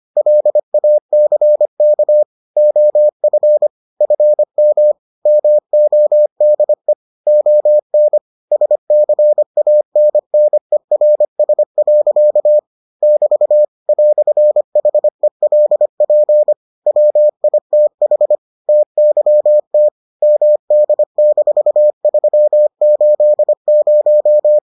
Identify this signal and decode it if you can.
Morse code, heard as a keyed tone, about 25 wpm, 600 Hz. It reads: LACK OF FM MODE ON SCANNERS. <BT> "HELP WITH TYT MD-380